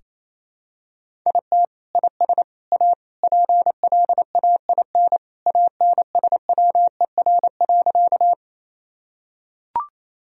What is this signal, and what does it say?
Morse code, heard as a keyed tone, about 28 wpm, 700 Hz. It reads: IT IS A PLAIN ANSWER.
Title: Sentences from Top 1000 Words - Review Part 2 - 28wpm